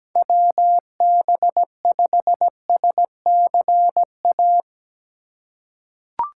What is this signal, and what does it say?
Morse code, heard as a keyed tone, about 17 wpm, 700 Hz. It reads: WB5SCA
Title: USA Call Signs - 6 Characters - 3x Character Spacing - 17wpm